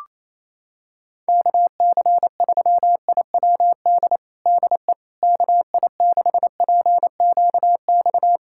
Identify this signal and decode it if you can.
Morse code, heard as a keyed tone, about 28 wpm, 700 Hz. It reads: KC3IWD DE KI6PQX
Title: USA Callsign DE Callsign - 6 Characters - 28wpm